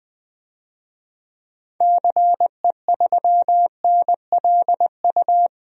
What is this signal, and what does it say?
Morse code, heard as a keyed tone, about 20 wpm, 700 Hz. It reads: CE3NLU
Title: DXCC-100 Call Signs - 6 Characters - 20wpm